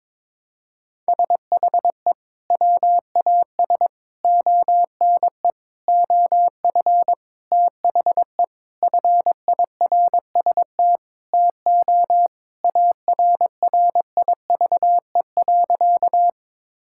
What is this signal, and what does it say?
Morse code, heard as a keyed tone, about 22 wpm, 700 Hz. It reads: SHE WAS ONE OF THE FIRST TO ARRIVE.
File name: practice / 22wpm / Sentences with 4-Word Phrases - 22wpm